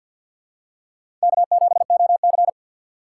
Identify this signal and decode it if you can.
Morse code, heard as a keyed tone, about 52 wpm, 700 Hz. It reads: K7OC
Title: USA Call Signs - 6x Character Spacing - 50wpm